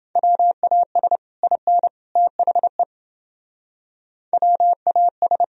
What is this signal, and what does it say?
Morse code, heard as a keyed tone, about 29 wpm, 700 Hz. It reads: WAS IN THE WAS